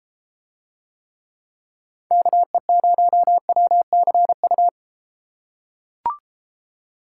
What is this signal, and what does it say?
Morse code, heard as a keyed tone, about 31 wpm, 700 Hz. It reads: KE0WCU